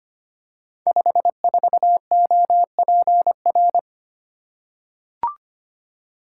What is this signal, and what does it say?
Morse code, heard as a keyed tone, about 25 wpm, 700 Hz. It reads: 54OPR
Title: Code Groups - 5 Letter:Numbers - 25wpm